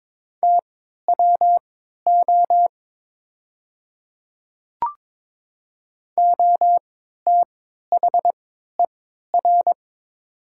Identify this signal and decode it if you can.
Morse code, heard as a keyed tone, about 22 wpm, spaced as for 13 wpm, 700 Hz. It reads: TWO OTHER
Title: 3x Character Spacing - Top 100 Words - 22wpm